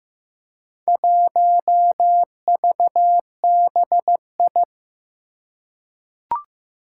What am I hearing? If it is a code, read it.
Morse code, heard as a keyed tone, about 15 wpm, 700 Hz. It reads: 1VBI
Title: Four Challenging Letter-Number Combinations 15wpm